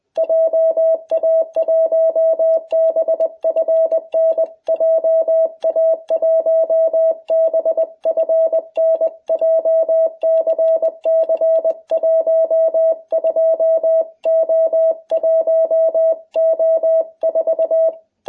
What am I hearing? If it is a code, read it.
Morse code, heard as a keyed tone, about 20 wpm, 660 Hz. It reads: JA1BFN JA1BFN JCC 12O1O4